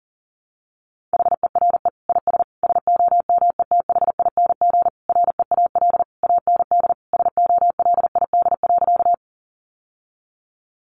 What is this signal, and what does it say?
Morse code, heard as a keyed tone, about 40 wpm, 700 Hz. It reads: HERE IS SOMETHING REAL AND SOLID.
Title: Sentences from Top 1500 Words 40wpm